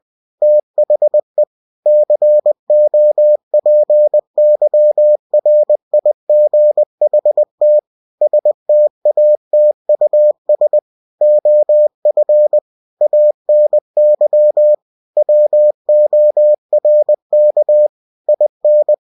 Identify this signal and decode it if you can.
Morse code, heard as a keyed tone, about 20 wpm, 600 Hz. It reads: THE COPYRIGHT STATUS OF ANY WORK IN
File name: cw-books / winnie-the-pooh / winnie-20wpm-page-0114